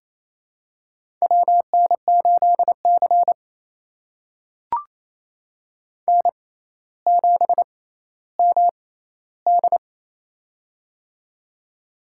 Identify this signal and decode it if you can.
Morse code, heard as a keyed tone, about 28 wpm, spaced as for 14 wpm, 700 Hz. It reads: WN8C N7MD
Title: USA Call Signs - 6x Character Spacing - 28wpm